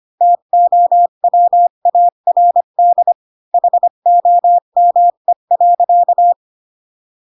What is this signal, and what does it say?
Morse code, heard as a keyed tone, about 25 wpm, 700 Hz. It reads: TOWARD HOME.